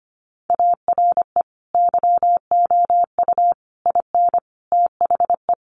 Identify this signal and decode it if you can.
Morse code, heard as a keyed tone, about 25 wpm, 700 Hz. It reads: ARE YOU IN THE
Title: Sentences from Top 1200 Words 25wpm